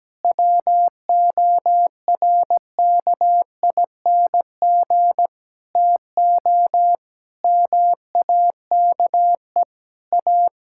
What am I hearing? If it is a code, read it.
Morse code, heard as a keyed tone, about 17 wpm, 700 Hz. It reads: WORKING TO MAKE A
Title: Sentences from Top 1000 Words - Review Part 6 - 17wpm